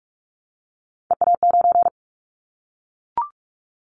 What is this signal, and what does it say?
Morse code, heard as a keyed tone, about 44 wpm, 700 Hz. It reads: EA9
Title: Callsign Prefixes for Top 100 DXCC - 45wpm